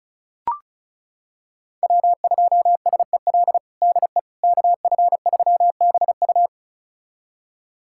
Morse code, heard as a keyed tone, about 36 wpm, 700 Hz. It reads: W2SEL DE KF3BU
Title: USA Callsign DE Callsign - 5 Characters - 35wpm